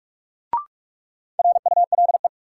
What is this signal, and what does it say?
Morse code, heard as a keyed tone, about 44 wpm, 700 Hz. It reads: RULE
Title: Sentences from Top 400 Words - New Word Drill - 45wpm